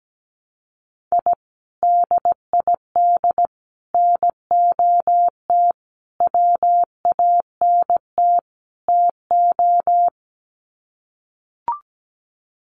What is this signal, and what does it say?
Morse code, heard as a keyed tone, about 17 wpm, 700 Hz. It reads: I DID NOT WANT TO
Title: Sets of 5 Words Encore 17wpm